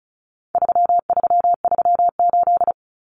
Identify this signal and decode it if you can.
Morse code, heard as a keyed tone, about 34 wpm, 700 Hz. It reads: 3338
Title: Common Cut Numbers - 4-Digit 35wpm